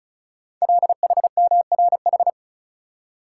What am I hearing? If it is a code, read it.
Morse code, heard as a keyed tone, about 36 wpm, 700 Hz. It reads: LHMRH
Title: Code Groups - 5 Letters - 35wpm